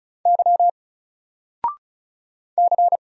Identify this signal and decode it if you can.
Morse code, heard as a keyed tone, about 36 wpm, 700 Hz. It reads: Y C
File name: Single-Letter-Number-35wpm